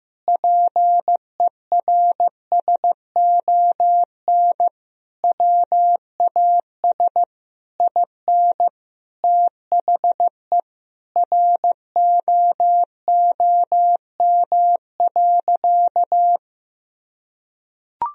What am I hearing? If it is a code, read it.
Morse code, heard as a keyed tone, about 15 wpm, 700 Hz. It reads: PERSON WAS IN THE ROOM.
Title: Sentences from Top 1000 Words - Review Part 4 - 15wpm